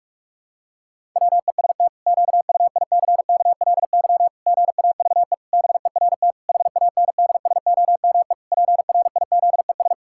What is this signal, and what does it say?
Morse code, heard as a keyed tone, about 46 wpm, 700 Hz. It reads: WEST QUICKLY GAVE BERT HANDSOME PRIZES
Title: Pangram Challenge v2 - 45wpm